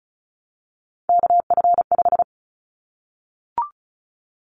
Morse code, heard as a keyed tone, about 34 wpm, 700 Hz. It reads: KF5
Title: Callsign Prefixes for Top 25 DXCC - Repeated 3x - 35wpm